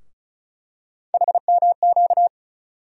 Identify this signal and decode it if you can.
Morse code, heard as a keyed tone, about 36 wpm, 700 Hz. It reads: HMQ
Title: USA Call Sign Postfixes - 3 Letters 35wpm